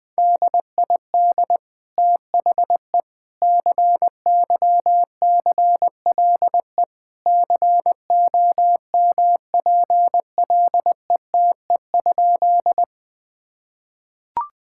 Morse code, heard as a keyed tone, about 21 wpm, 700 Hz. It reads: DID THE CYCLE COMPLETE?
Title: Sentences from Top 1000 Words - Review Part 6 - 20wpm